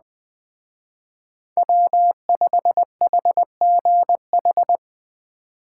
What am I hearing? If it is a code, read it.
Morse code, heard as a keyed tone, about 20 wpm, 700 Hz. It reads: W5HGH